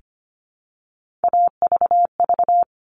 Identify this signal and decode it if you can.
Morse code, heard as a keyed tone, about 25 wpm, 700 Hz. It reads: AVV
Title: All Cut Numbers - 3-Digit 25wpm